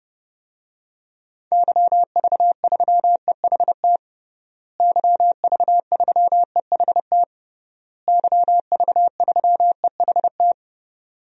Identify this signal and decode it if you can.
Morse code, heard as a keyed tone, about 32 wpm, 700 Hz. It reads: YV3EHT YV3EHT YV3EHT
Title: DXCC-100 Call Signs - 6 Characters - Repeated 3x - 30wpm